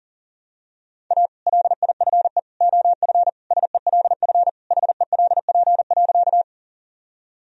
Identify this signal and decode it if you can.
Morse code, heard as a keyed tone, about 40 wpm, 700 Hz. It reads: A LIFE OF SELF HELP.
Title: Sentences from Top 1000 Words - Review Part 1 - 40wpm